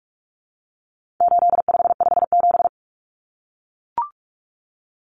Morse code, heard as a keyed tone, about 44 wpm, 700 Hz. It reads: 8557
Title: Numbers - 4-Digit 45wpm